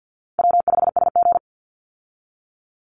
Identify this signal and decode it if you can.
Morse code, heard as a keyed tone, about 50 wpm, 700 Hz. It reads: W5SG
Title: Call Signs - 50wpm